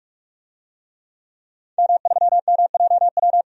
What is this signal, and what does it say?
Morse code, heard as a keyed tone, about 46 wpm, 700 Hz. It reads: M3MJW
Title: DXCC-100 Call Signs - 5 Characters - 45wpm